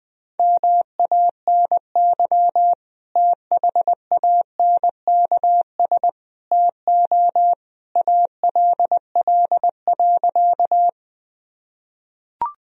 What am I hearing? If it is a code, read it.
Morse code, heard as a keyed tone, about 21 wpm, 700 Hz. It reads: MANY THANKS TO ALL.
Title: Sentences from Top 1000 Words - Review Part 3 - 20wpm